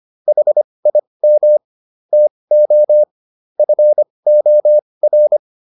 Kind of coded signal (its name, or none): Morse code